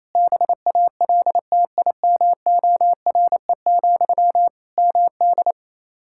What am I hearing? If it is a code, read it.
Morse code, heard as a keyed tone, about 28 wpm, 700 Hz. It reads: BALTIMORE, MD
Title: Practice QSO - Exchange 1 - Beginning - 28wpm